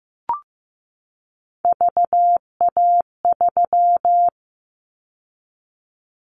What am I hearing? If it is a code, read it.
Morse code, heard as a keyed tone, about 15 wpm, 700 Hz. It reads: VA3
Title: Callsign Prefixes for Top 25 DXCC - 15wpm